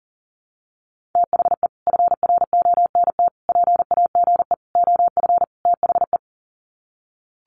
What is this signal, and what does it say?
Morse code, heard as a keyed tone, about 40 wpm, 700 Hz. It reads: THE FRONT PAGE OF THE
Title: Sets of 5 Words Encore 40wpm